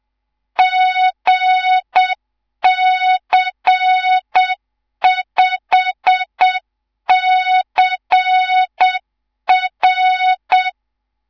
Morse code, heard as a keyed tone, about 7 wpm, 3700 Hz. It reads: GC5CR
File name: Morze-HA2MN / hang / 11 GYAKORLAT